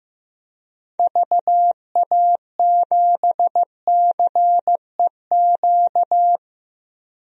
Morse code, heard as a keyed tone, about 15 wpm, 700 Hz. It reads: VA7CEQ
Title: DXCC-100 Call Signs - 6 Characters - 15wpm